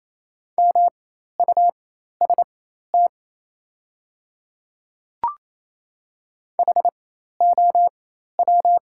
Morse code, heard as a keyed tone, about 28 wpm, spaced as for 13 wpm, 690 Hz. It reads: MUST HOW
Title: 4x Character Spacing - Top 200 Words - 28wpm